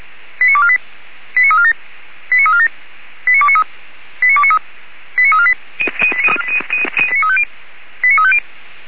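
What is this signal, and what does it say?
Five-tone selective call (ZVEI), heard as quick runs of five tones, tones of about 70 ms. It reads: 89248 89347 89347 89293 89293 89348 89350
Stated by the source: ZVEI five-tone selective call received by SDR.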